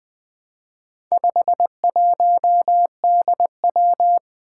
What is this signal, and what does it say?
Morse code, heard as a keyed tone, about 20 wpm, 700 Hz. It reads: 51DW
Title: Four Challenging Letter-Number Combinations 20wpm